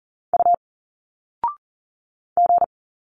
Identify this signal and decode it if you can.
Morse code, heard as a keyed tone, about 40 wpm, 700 Hz. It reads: U G
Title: Single Letters 40wpm